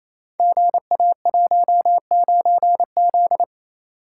Morse code, heard as a keyed tone, about 27 wpm, 700 Hz. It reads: GA19Z
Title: Code Groups - 5 Letter:Numbers - 28wpm